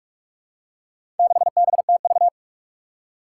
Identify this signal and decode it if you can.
Morse code, heard as a keyed tone, about 46 wpm, 700 Hz. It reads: 6BTV
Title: Four Challenging Letter-Number Combinations 45wpm